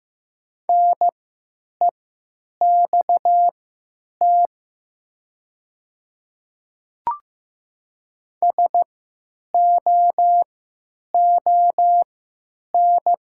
Morse code, heard as a keyed tone, about 15 wpm, spaced as for 9 wpm, 700 Hz. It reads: NEXT SOON
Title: 3x Character Spacing - Top 500 Words - 15wpm